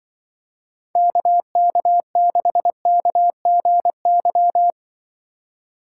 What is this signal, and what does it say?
Morse code, heard as a keyed tone, about 24 wpm, 700 Hz. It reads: KK6KGY
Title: Speed Racing - USA 2x3 Call Signs 20wpm